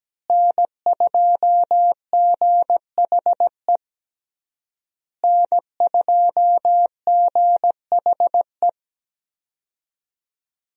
Morse code, heard as a keyed tone, about 17 wpm, 700 Hz. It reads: N2GHE N2GHE
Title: USA Call Signs - 5 Characters - Repeated 3x - 17wpm